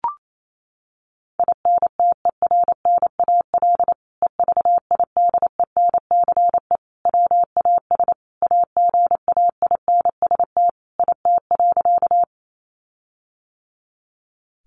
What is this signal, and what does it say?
Morse code, heard as a keyed tone, about 28 wpm, 700 Hz. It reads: INTERNAL EVIDENCE WAS AGAINST IT.
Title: Sentences from Top 1400 Words 28wpm